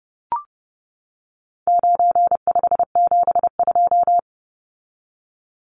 Morse code, heard as a keyed tone, about 30 wpm, 700 Hz. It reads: 9572
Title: Numbers - 4-Digit 30wpm